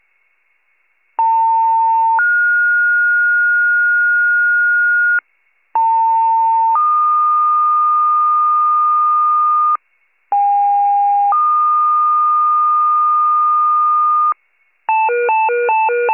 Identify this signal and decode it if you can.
Two-tone page, heard as a one-second tone, then a three-second tone, about 900 then 1430 Hz, about 900 then 1220 Hz, about 810 then 1220 Hz. It is heard from 1.2 to 14.3 s.
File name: wcalert